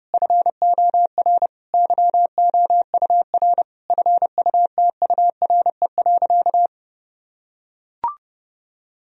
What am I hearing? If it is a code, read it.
Morse code, heard as a keyed tone, about 32 wpm, 700 Hz. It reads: FOR YOUR FUTURE.
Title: Sentences from Top 1000 Words - Review Part 4 - 30wpm